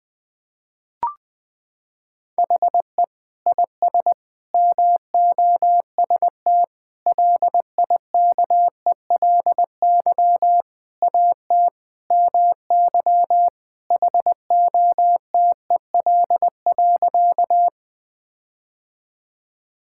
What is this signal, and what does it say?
Morse code, heard as a keyed tone, about 21 wpm, 700 Hz. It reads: HE IS MOST LIKELY AT MY HOTEL.
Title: Sentences from Top 1600 Words 20wpm